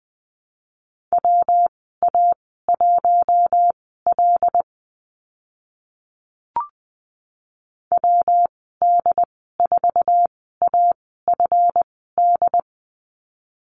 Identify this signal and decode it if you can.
Morse code, heard as a keyed tone, about 20 wpm, spaced as for 14 wpm, 700 Hz. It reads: WA1L WD4AFD
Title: 140 Lesson 23g - Letter f - Call Signs 20wpm